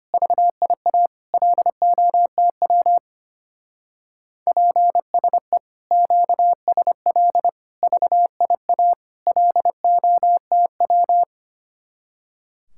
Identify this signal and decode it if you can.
Morse code, heard as a keyed tone, about 32 wpm, 700 Hz. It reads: VIA LOTW PSE QSL VIA LOTW